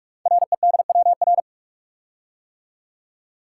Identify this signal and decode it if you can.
Morse code, heard as a keyed tone, about 44 wpm, 700 Hz. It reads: REDWR
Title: Code Groups - 5 Letters - 45wpm